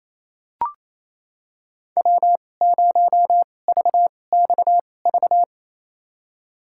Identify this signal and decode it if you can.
Morse code, heard as a keyed tone, about 28 wpm, spaced as for 20 wpm, 700 Hz. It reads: W0VXV
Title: USA Call Signs - 5 Characters - 2x Character Spacing - 28wpm